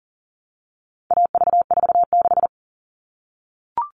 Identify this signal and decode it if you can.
Morse code, heard as a keyed tone, about 40 wpm, 700 Hz. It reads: AV46
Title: All Cut Numbers - 4-Digit 40wpm